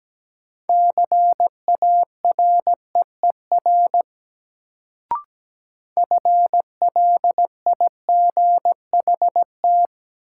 Morse code, heard as a keyed tone, about 17 wpm, 700 Hz. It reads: CAREER FLIGHT
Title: Sentences from Top 1000 Words - New Word Drill - 17wpm